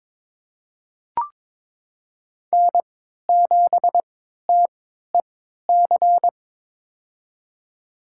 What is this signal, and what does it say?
Morse code, heard as a keyed tone, about 22 wpm, spaced as for 12 wpm, 700 Hz. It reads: N7TEC
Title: USA Call Signs - 5 Characters - 3x Character Spacing - 22wpm